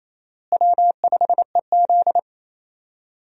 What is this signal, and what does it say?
Morse code, heard as a keyed tone, about 28 wpm, 700 Hz. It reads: W5EZ
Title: USA Call Signs - 4 Characters - Repeated 3x - 28wpm